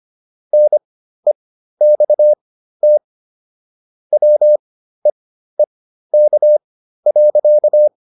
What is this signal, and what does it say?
Morse code, heard as a keyed tone, about 25 wpm, spaced as for 13 wpm, 600 Hz. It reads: NEXT WEEK.